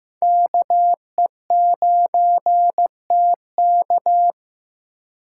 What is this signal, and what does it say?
Morse code, heard as a keyed tone, about 15 wpm, 700 Hz. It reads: KE9TK
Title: USA Call Signs - 2x Character Spacing - 15wpm